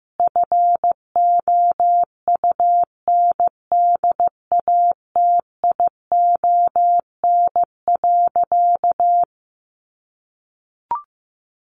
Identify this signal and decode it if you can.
Morse code, heard as a keyed tone, about 15 wpm, 700 Hz. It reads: FOUNDATION.
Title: Sentences from Top 1200 Words 15wpm